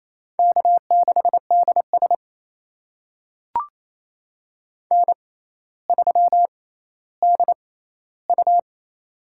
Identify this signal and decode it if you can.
Morse code, heard as a keyed tone, about 28 wpm, spaced as for 14 wpm, 700 Hz. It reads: K6DS N3DU
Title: USA Call Signs - 6x Character Spacing - 28wpm